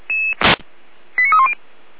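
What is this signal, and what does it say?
Five-tone selective call (ZVEI), heard as quick runs of five tones, tones of about 70 ms. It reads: 89310
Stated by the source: ZVEI five-tone selective call received with an SDR and recorded by Feuerwehr Orlamünde.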